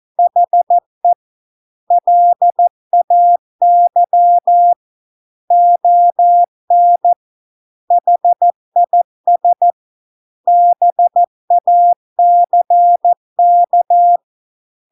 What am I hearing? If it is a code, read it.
Morse code, heard as a keyed tone, about 14 wpm, 700 Hz. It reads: HE LAY ON HIS BACK